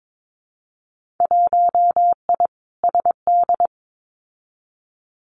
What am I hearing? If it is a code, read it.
Morse code, heard as a keyed tone, about 22 wpm, 700 Hz. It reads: 1I SD
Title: Winter Field Day Exchanges 22wpm